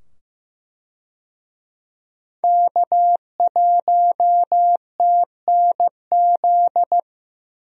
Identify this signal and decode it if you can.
Morse code, heard as a keyed tone, about 15 wpm, 700 Hz. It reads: K1TNZ
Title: USA Call Signs - 5 Characters - 15wpm